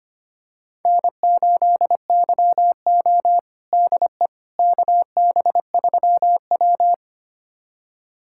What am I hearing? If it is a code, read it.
Morse code, heard as a keyed tone, about 25 wpm, 700 Hz. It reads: N8YO DE KB3W
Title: USA Callsign DE Callsign - 4 Characters - 25wpm